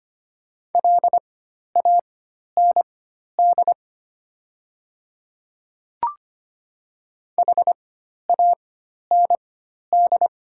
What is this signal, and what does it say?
Morse code, heard as a keyed tone, about 25 wpm, spaced as for 12 wpm, 700 Hz. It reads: LAND HAND
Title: 4x Character Spacing - Top 200 Words - 25wpm